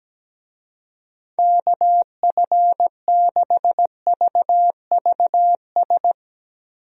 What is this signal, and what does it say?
Morse code, heard as a keyed tone, about 17 wpm, 700 Hz. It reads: KF6VVS
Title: USA Call Signs - 6 Characters - Repeated 3x - 17wpm